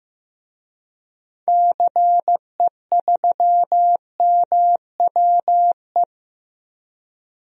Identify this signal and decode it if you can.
Morse code, heard as a keyed tone, about 15 wpm, 700 Hz. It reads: CE3MWE
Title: DXCC-100 Call Signs - 6 Characters - 15wpm